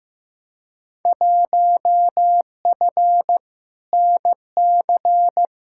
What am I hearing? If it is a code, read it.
Morse code, heard as a keyed tone, about 15 wpm, 700 Hz. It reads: 1F NC
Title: Field Day Exchanges - 15wpm